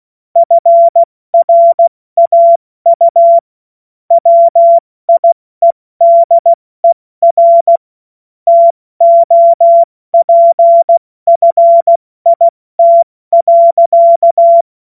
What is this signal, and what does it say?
Morse code, heard as a keyed tone, about 16 wpm, 680 Hz. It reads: FRAU WIEDER TOPFIT.
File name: Morsekurs / Witze / Witz 02 April 2024 70 F80